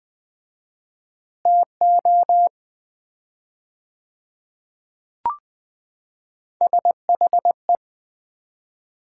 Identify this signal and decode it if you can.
Morse code, heard as a keyed tone, about 20 wpm, 700 Hz. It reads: TO SHE
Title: 4x Word-Spacing - Sets of 3 Words - 20wpm